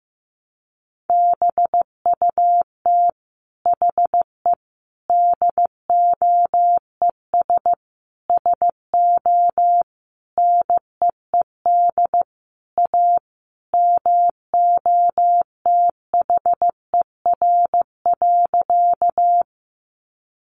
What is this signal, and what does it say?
Morse code, heard as a keyed tone, about 15 wpm, 700 Hz. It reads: BUT HE DOES SO NEED A MOTHER.
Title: Sentences from Top 200 Words 15wpm